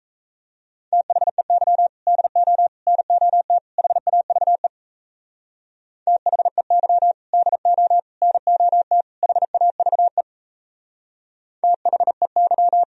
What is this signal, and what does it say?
Morse code, heard as a keyed tone, about 35 wpm, 700 Hz. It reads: THEY DO NOT HAVE THEY DO NOT HAVE THEY
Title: Speed-Racing - Sets of 4 Words 28wpm